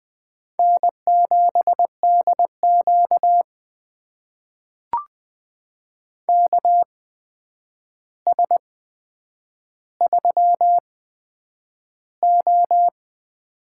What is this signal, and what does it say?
Morse code, heard as a keyed tone, about 20 wpm, spaced as for 9 wpm, 700 Hz. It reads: N7DQ KS3O
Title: USA Call Signs - 4 Characters - 8x Character Spacing - 20wpm